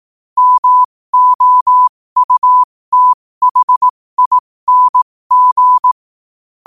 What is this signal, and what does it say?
Morse code, heard as a keyed tone, about 18 wpm, spaced as for 15 wpm, 1000 Hz. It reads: MOUTHING
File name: qotdfiles / 15wpm